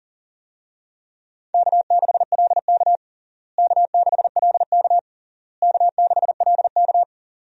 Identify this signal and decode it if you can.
Morse code, heard as a keyed tone, about 40 wpm, 700 Hz. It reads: K6LK K6LK K6LK
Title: USA Call Signs - 4 Characters - Repeated 3x - 40wpm